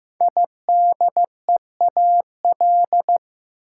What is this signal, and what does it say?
Morse code, heard as a keyed tone, about 15 wpm, 700 Hz. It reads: IDEAL
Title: Sentences from Top 1400 Words - New Word Drill - 15wpm